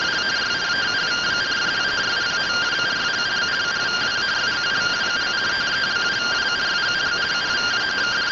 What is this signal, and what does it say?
Two-tone teleprinter signal, about 170 Hz shift.